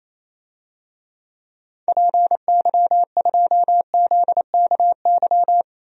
Morse code, heard as a keyed tone, about 28 wpm, 700 Hz. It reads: PY2ZKY
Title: DXCC-100 Call Signs - 6 Characters - Repeated 3x - 28wpm